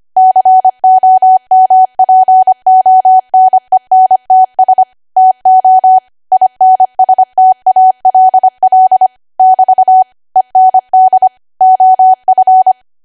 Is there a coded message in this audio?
Yes: Morse code from 0.2 to 12.7 s, heard as a keyed tone, about 25 wpm, 750 Hz. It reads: COMPONENTS TO INSTALL <BT> END OF